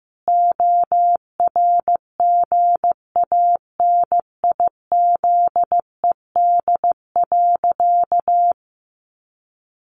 Morse code, heard as a keyed tone, about 15 wpm, 700 Hz. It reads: ORGANIZED.
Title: Sentences from Top 1400 Words 15wpm